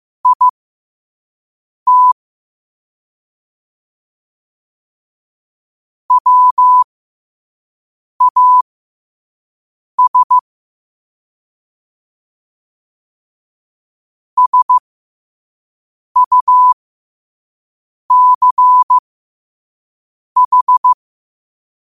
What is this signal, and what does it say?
Morse code, heard as a keyed tone, about 15 wpm, spaced as for 5 wpm, 1000 Hz. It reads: IT WAS SUCH
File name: qotdfiles / 5wpm